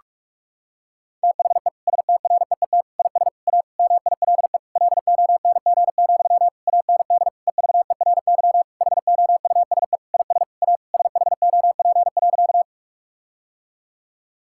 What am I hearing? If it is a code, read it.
Morse code, heard as a keyed tone, about 46 wpm, 700 Hz. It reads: THE STREET IS A MILE LONG, AND EVERY HOUSE IS A SHOP.
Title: Sentences from Top 1600 Words 45wpm